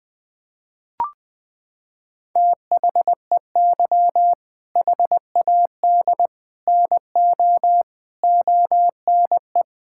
Morse code, heard as a keyed tone, about 20 wpm, 700 Hz. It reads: THEY HAD NO ONE
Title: Sentences from Top 1600 Words 20wpm